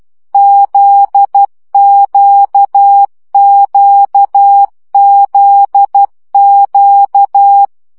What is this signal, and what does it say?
Morse code, heard as a keyed tone, about 12 wpm, 800 Hz. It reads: ZQQZQ
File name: CW gyakorlas g z q 10 baud